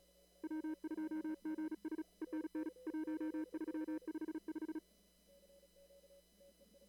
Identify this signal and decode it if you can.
Morse code, heard as a keyed tone, about 35 wpm, 310 Hz. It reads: W2GS ENN 1355
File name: CW / kt5j wpx cw-op-k5pi